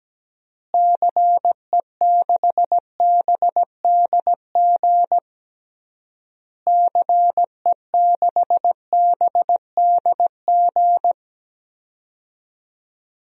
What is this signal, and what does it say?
Morse code, heard as a keyed tone, about 17 wpm, 700 Hz. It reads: CE6BDG CE6BDG
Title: DXCC-100 Call Signs - 6 Characters - Repeated 3x - 17wpm